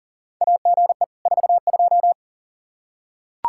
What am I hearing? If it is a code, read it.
Morse code, heard as a keyed tone, about 40 wpm, 700 Hz. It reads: AGE 42
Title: Practice QSO - Exchange 3 - More Info - 40wpm